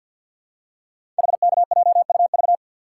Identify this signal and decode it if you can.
Morse code, heard as a keyed tone, about 51 wpm, 700 Hz. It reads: HKJUV
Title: Code Groups - 5 Letters - 50wpm